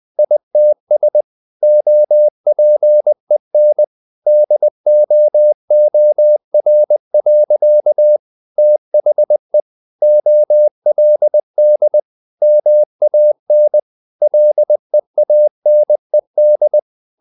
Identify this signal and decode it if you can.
Morse code, heard as a keyed tone, about 20 wpm, 600 Hz. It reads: ITS OPEN DOOR. THE OLD MAN LEANED
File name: the old man and the sea / the old man and the sea part 1 20wpm